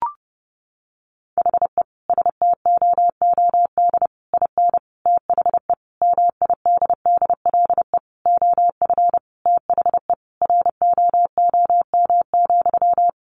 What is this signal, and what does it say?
Morse code, heard as a keyed tone, about 30 wpm, 700 Hz. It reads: HE STOOD IN THE MIDDLE OF THE ROOM,
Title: Sentences from Top 1400 Words 30wpm